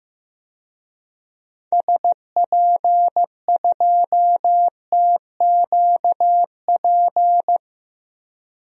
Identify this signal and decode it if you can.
Morse code, heard as a keyed tone, about 15 wpm, 700 Hz. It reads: SP2TQP
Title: DXCC-100 Call Signs - 6 Characters - 15wpm